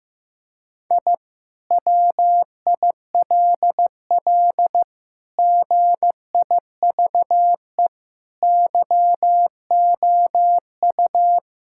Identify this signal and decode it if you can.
Morse code, heard as a keyed tone, about 15 wpm, 700 Hz. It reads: I WILL GIVE YOU
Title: Sentences from Top 1000 Words - Review Part 6 - 15wpm